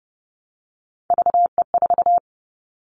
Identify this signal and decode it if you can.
Morse code, heard as a keyed tone, about 30 wpm, 700 Hz. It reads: VE4
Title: Callsign Prefixes for Top 100 DXCC - 30wpm